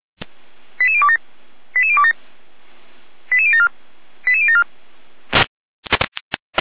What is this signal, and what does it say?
Five-tone selective call (ZVEI), heard as quick runs of five tones, tones of about 70 ms. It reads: 88027 88027 88074 88074
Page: /455/SDR/ZVEI/